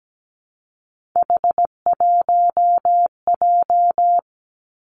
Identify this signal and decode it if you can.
Morse code, heard as a keyed tone, about 17 wpm, 700 Hz. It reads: H1J
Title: Three Challenging Letter-Number Combinations 17wpm